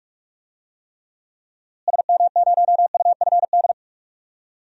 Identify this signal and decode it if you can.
Morse code, heard as a keyed tone, about 44 wpm, 700 Hz. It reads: SM0UFD
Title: DXCC-100 Call Signs - 6 Characters - Repeated 3x - 45wpm